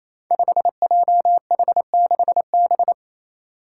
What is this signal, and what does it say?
Morse code, heard as a keyed tone, about 27 wpm, 700 Hz. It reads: 5JH6B
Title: Five Challenging Letter-Number Combinations 28wpm